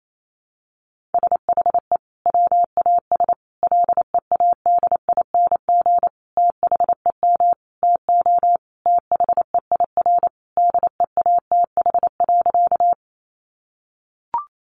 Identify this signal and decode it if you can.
Morse code, heard as a keyed tone, about 28 wpm, 700 Hz. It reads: SHE WAS LEADING THEM TO THEIR DEATH.
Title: Sentences from Top 1100 Words 28wpm